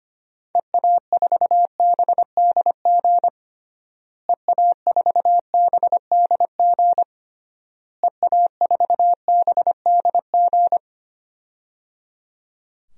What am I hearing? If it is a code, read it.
Morse code, heard as a keyed tone, about 25 wpm, 700 Hz. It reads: EA4BDG EA4BDG EA4BDG
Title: DXCC-100 Call Signs - 6 Characters - Repeated 3x - 25wpm